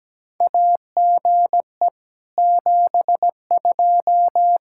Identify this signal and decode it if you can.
Morse code, heard as a keyed tone, about 17 wpm, 700 Hz. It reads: AGE 72
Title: K1USN SST QSO Practice 17wpm